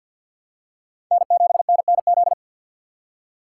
Morse code, heard as a keyed tone, about 48 wpm, 700 Hz. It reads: N7NNZ
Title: USA Call Signs - 5 Characters - 6x Character Spacing - 50wpm